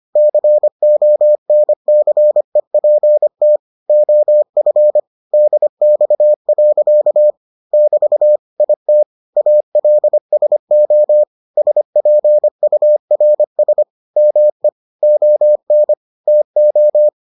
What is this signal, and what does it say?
Morse code, heard as a keyed tone, about 25 wpm, 600 Hz. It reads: CONCEPT OF DX. <BT> IT ALSO SPURS ME ON TO